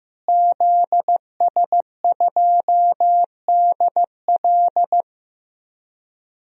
Morse code, heard as a keyed tone, about 15 wpm, 700 Hz. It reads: ZS2DL
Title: DXCC-100 Call Signs - 5 Characters - Repeated 2x - 15wpm